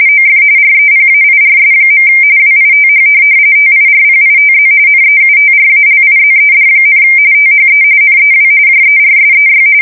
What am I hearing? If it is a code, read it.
RTTY, about 45.45 baud. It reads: OF REPAIRS. WITH TIRPITZ NEUTRALIZED, THE NEW JERSEY
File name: fsk170 / USS-New-Jersey